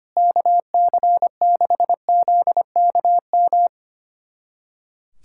Morse code, heard as a keyed tone, about 25 wpm, 700 Hz. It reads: KC6ZKM
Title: USA Callsign DE Callsign - 6 Characters - 25wpm